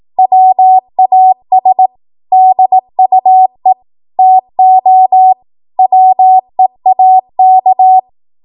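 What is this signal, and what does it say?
Morse code, heard as a keyed tone, about 18 wpm, 750 Hz. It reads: WAS DUE TO WEAK
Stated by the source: 18 WPM Morse code practice from a national amateur radio association's code archive.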